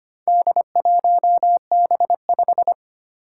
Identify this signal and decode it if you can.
Morse code, heard as a keyed tone, about 25 wpm, 700 Hz. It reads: D1B5
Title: Four Challenging Letter-Number Combinations 25wpm